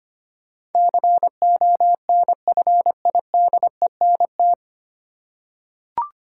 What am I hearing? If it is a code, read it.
Morse code, heard as a keyed tone, about 25 wpm, 700 Hz. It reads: CONFIDENT
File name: practice / 25wpm / 9 Letter Words - First 3 Letters Given - 25wpm